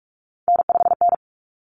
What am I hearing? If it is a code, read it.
Morse code, heard as a keyed tone, about 44 wpm, 700 Hz. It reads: N5N
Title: Common Cut Numbers - 3-Digit 45wpm